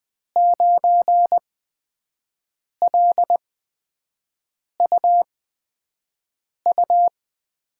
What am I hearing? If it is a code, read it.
Morse code, heard as a keyed tone, about 20 wpm, spaced as for 5 wpm, 700 Hz. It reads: 9LUU
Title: USA Call Signs - 5 Characters - 8x Character Spacing - 20wpm